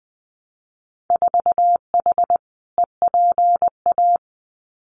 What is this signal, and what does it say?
Morse code, heard as a keyed tone, about 20 wpm, 700 Hz. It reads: 4H EPA
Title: Winter Field Day Exchanges 20wpm